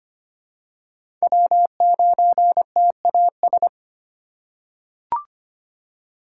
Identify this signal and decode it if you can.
Morse code, heard as a keyed tone, about 25 wpm, 700 Hz. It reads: W9TAS
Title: USA Call Signs - 4x Character Spacing - 25wpm